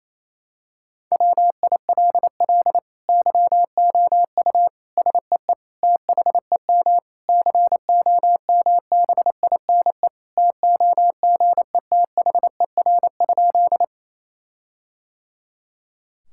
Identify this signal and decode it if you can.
Morse code, heard as a keyed tone, about 28 wpm, 700 Hz. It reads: WILL YOU SEE THEM COMBINE TOGETHER?
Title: Sentences from Top 1600 Words 28wpm